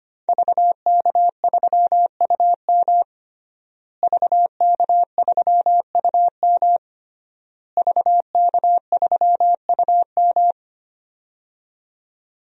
Morse code, heard as a keyed tone, about 25 wpm, 700 Hz. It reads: VK3UM VK3UM VK3UM
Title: DXCC-100 Call Signs - 5 Characters - Repeated 3x - 25wpm